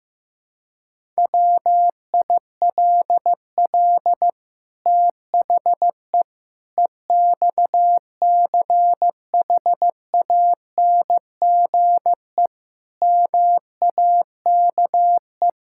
Morse code, heard as a keyed tone, about 15 wpm, 700 Hz. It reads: WILL THE EXCHANGE MAKE
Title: Sentences from Top 1000 Words - Review Part 6 - 15wpm